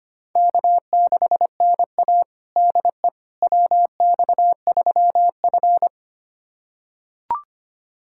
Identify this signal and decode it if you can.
Morse code, heard as a keyed tone, about 25 wpm, 700 Hz. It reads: K6NA DE WX3F
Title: USA Callsign DE Callsign - 4 Characters - 25wpm